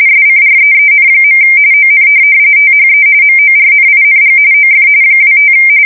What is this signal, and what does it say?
RTTY, about 45.45 baud. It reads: SCREEN. ON THE MORNING OF 25
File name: fsk170 / USS-Massachusetts